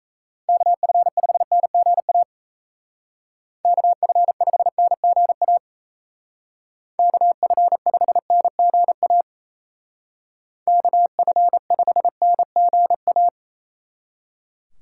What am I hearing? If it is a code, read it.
Morse code, heard as a keyed tone, about 33 wpm, 700 Hz. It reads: KF5NGA KF5NGA KF5NGA KF5NGA
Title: Speed Racing - USA 2x3 Call Signs 28wpm